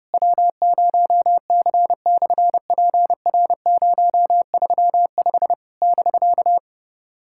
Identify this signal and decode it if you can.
Morse code, heard as a keyed tone, about 31 wpm, 700 Hz. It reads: W0C/PR035 <BK>
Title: SOTA QSO Practice 30wpm